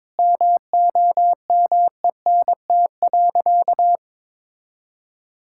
Morse code, heard as a keyed tone, about 22 wpm, 700 Hz. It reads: MOMENT.
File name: practice / 22wpm / Sentences with 3-Word Phrases - 22wpm